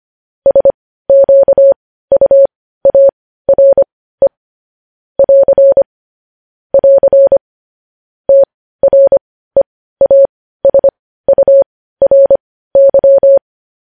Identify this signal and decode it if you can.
Morse code, heard as a keyed tone, about 25 wpm, spaced as for 15 wpm, 550 Hz. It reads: SQUARE <AR> <AR> TREASURY
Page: news-headlines-in-morse-code-at-15-wpm